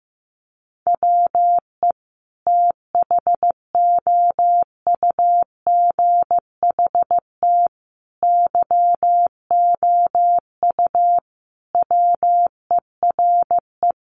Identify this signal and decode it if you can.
Morse code, heard as a keyed tone, about 15 wpm, 700 Hz. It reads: WE THOUGHT YOU WERE
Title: Sentences from Top 800 Words 15wpm